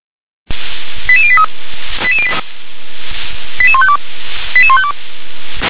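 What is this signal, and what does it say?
Five-tone selective call (ZVEI), heard as quick runs of five tones, tones of about 70 ms. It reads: 88083 88152 88152
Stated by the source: ZVEI five-tone selective call, code 88702, received on an SDR.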